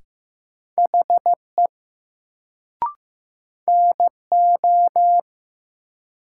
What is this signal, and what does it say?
Morse code, heard as a keyed tone, about 15 wpm, 700 Hz. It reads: HE NO
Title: Sentences from Top 100 Words - New Word Drill - 15wpm